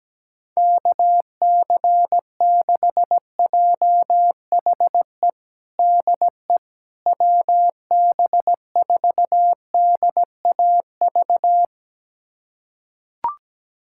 Morse code, heard as a keyed tone, about 17 wpm, 700 Hz. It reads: KC6JHE DE WB4DAV
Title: USA Callsign DE Callsign 17wpm